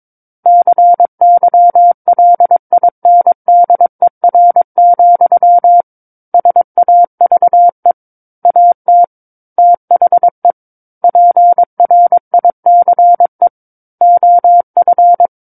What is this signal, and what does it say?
Morse code, heard as a keyed tone, about 22 wpm, 700 Hz. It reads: CYLINDER, SAVE AT THE PRICE OF
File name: Book One / 13-Chapter 13 - 22 WPM